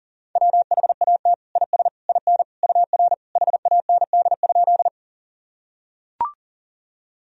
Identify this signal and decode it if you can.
Morse code, heard as a keyed tone, about 40 wpm, 700 Hz. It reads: WHAT IS IN UR HAND?
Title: Sentences from Top 200 Words and Abbreviations 40wpm